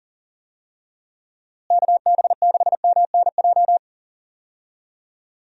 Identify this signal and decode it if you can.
Morse code, heard as a keyed tone, about 40 wpm, 700 Hz. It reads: KB6MNJ
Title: USA Call Signs - 6 Characters - Repeated 2x - 40wpm